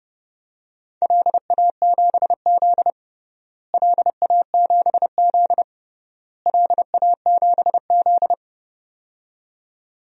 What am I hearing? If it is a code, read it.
Morse code, heard as a keyed tone, about 32 wpm, 700 Hz. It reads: LA7Z LA7Z LA7Z
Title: DXCC-100 Call Signs - 4 Characters - Repeated 3x - 30wpm